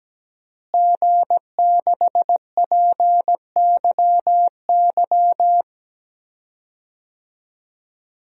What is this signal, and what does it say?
Morse code, heard as a keyed tone, about 17 wpm, 700 Hz. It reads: G6PYY
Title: DXCC-100 Call Signs - 5 Characters - Repeated 3x - 17wpm